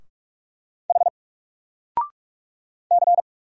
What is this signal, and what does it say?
Morse code, heard as a keyed tone, about 44 wpm, 700 Hz. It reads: H C
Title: Single Character (Letter:Number:Punctuation:Prosign) 45wpm